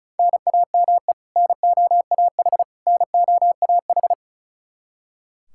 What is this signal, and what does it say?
Morse code, heard as a keyed tone, about 34 wpm, 700 Hz. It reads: NAME NOAH NOAH
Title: Practice QSO - Exchange 1 - Beginning - 35wpm